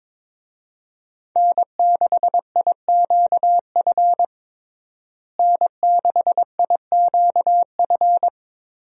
Morse code, heard as a keyed tone, about 22 wpm, 700 Hz. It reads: N6IQF N6IQF
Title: USA Call Signs - 5 Characters - Repeated 3x - 22wpm